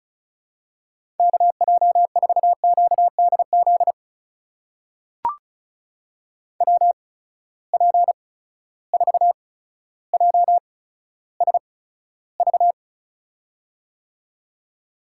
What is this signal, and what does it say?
Morse code, heard as a keyed tone, about 36 wpm, spaced as for 15 wpm, 700 Hz. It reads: KJ4QDZ WP4JSV
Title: USA Call Signs - 6 Characters - 8x Character Spacing - 35wpm